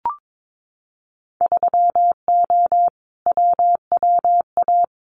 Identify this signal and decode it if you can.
Morse code, heard as a keyed tone, about 22 wpm, 700 Hz. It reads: 3O WWA
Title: Winter Field Day Exchanges Encore 22wpm